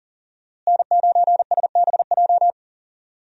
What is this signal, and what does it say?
Morse code, heard as a keyed tone, about 38 wpm, 700 Hz. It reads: N9SBJ
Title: USA Call Signs - 5 Characters - Repeated 3x - 40wpm